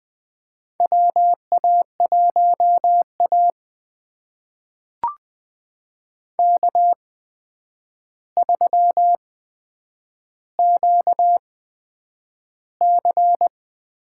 Morse code, heard as a keyed tone, about 20 wpm, spaced as for 9 wpm, 700 Hz. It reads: WA1A K3QC
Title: USA Call Signs - 4 Characters - 8x Character Spacing - 20wpm